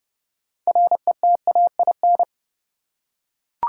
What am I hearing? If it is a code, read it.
Morse code, heard as a keyed tone, about 31 wpm, 700 Hz. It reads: RETAIN
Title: Six Letter Words - 30wpm